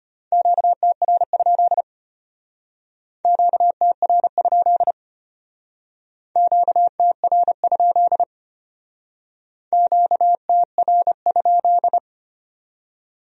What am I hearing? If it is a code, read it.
Morse code, heard as a keyed tone, about 31 wpm, 700 Hz. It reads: QTR? QTR? QTR? QTR?
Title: Speed-Racing Q-Codes - 25wpm